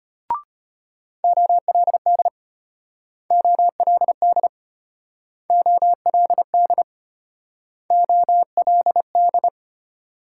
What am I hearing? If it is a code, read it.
Morse code, heard as a keyed tone, about 33 wpm, 700 Hz. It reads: OLD OLD OLD OLD